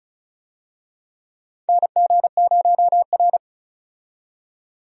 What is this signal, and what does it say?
Morse code, heard as a keyed tone, about 36 wpm, 700 Hz. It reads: NG0R
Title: USA Call Signs - 4 Characters - 35wpm